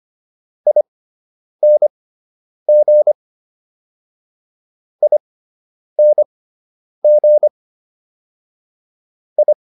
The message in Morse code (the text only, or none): ING ING I